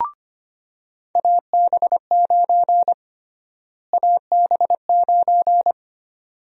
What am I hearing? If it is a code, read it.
Morse code, heard as a keyed tone, about 25 wpm, 700 Hz. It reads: AB9 AB9
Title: USA Call Sign Prefixes - Repeated 2x - 25wpm